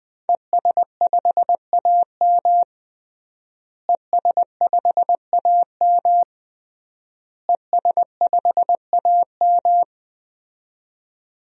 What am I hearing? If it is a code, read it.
Morse code, heard as a keyed tone, about 20 wpm, 700 Hz. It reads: ES5AM ES5AM ES5AM
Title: DXCC-100 Call Signs - 5 Characters - Repeated 3x - 20wpm